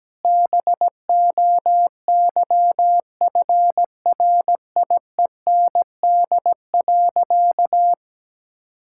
Morse code, heard as a keyed tone, about 17 wpm, 700 Hz. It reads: BOYFRIEND.
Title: Sentences from Top 1000 Words - Review Part 5 - 17wpm